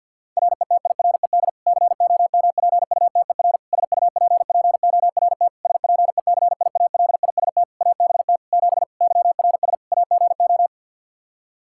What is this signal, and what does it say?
Morse code, heard as a keyed tone, about 51 wpm, 700 Hz. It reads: RETIRED COMPUTER SUPPORT SPECIALIST ABT 7 YRS AGO
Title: Practice QSO - Exchange 3 - More Info - 50wpm